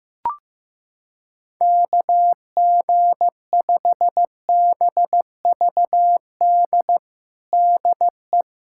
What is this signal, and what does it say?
Morse code, heard as a keyed tone, about 15 wpm, 700 Hz. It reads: KG5BVD DE
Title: USA Callsign DE Callsign 15wpm